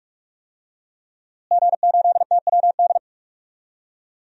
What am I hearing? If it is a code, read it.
Morse code, heard as a keyed tone, about 46 wpm, 700 Hz. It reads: G8TWD
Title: DXCC-100 Call Signs - 5 Characters - Repeated 2x - 45wpm